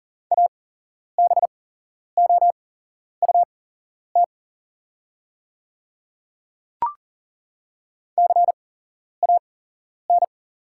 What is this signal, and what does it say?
Morse code, heard as a keyed tone, about 40 wpm, spaced as for 11 wpm, 700 Hz. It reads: ABOUT CAN
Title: 8x Character Spacing - Top 100 Words - 40wpm